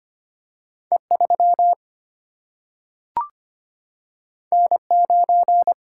Morse code, heard as a keyed tone, about 25 wpm, 700 Hz. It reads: E 3 N 9
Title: Single-Letter Single-Number 25wpm